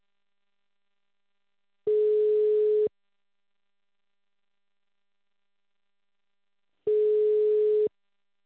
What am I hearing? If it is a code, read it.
Ringback tone (Europe, 425 Hz).